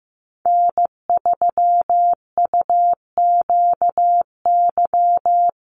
Morse code, heard as a keyed tone, about 15 wpm, 700 Hz. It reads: N3UQY
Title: USA Call Signs Repeated 3x 15wpm